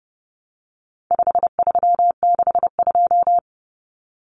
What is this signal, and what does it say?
Morse code, heard as a keyed tone, about 30 wpm, 700 Hz. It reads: 5362
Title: Common Cut Numbers - 4-Digit 30wpm